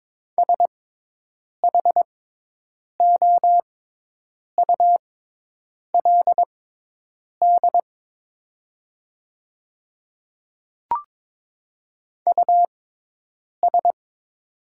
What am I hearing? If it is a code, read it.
Morse code, heard as a keyed tone, about 22 wpm, spaced as for 8 wpm, 700 Hz. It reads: SHOULD US
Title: 6x Character Spacing - Top 300 Words - 22wpm